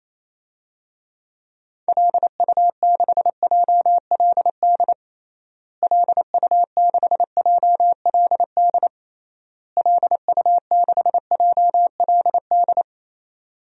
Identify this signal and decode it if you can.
Morse code, heard as a keyed tone, about 28 wpm, 700 Hz. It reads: LU6JLD LU6JLD LU6JLD
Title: DXCC-100 Call Signs - 6 Characters - Repeated 3x - 28wpm